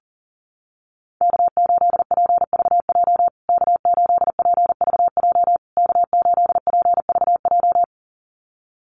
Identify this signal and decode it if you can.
Morse code, heard as a keyed tone, about 40 wpm, 700 Hz. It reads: K8PVJ K8PVJ K8PVJ
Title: USA Call Signs Repeated 3x 40wpm